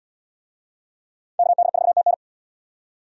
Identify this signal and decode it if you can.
Morse code, heard as a keyed tone, about 74 wpm, 700 Hz. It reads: KD4TN